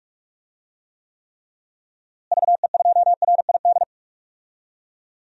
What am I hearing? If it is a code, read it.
Morse code, heard as a keyed tone, about 46 wpm, 700 Hz. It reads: VE2RID
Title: DXCC-100 Call Signs - 6 Characters - Repeated 3x - 45wpm